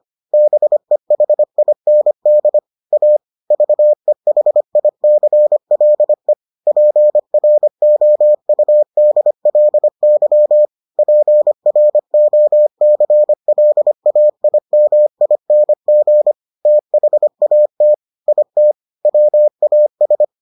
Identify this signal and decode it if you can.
Morse code, heard as a keyed tone, about 25 wpm, 600 Hz. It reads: BEHIND A VEHICLE PROUDLY PROCLAIMING THAT IT WAS